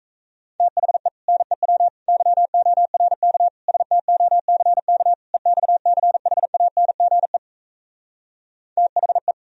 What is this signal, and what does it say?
Morse code, heard as a keyed tone, about 38 wpm, 700 Hz. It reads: THE NEW YORK STOCK EXCHANGE THE